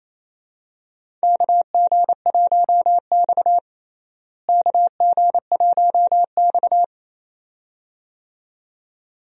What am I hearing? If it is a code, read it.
Morse code, heard as a keyed tone, about 28 wpm, 700 Hz. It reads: KG1X KG1X
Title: USA Call Signs - 4 Characters - Repeated 2x - 28wpm